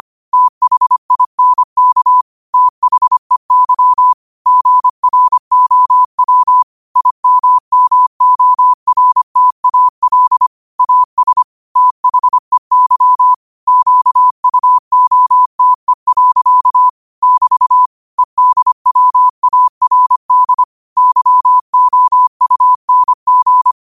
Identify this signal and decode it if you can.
Morse code, heard as a keyed tone, about 25 wpm, 1000 Hz. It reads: THINK THEY GROW IMMORTAL AS THEY QUOTE. <BT> EDWARD YOUNG
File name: qotdfiles / 25wpm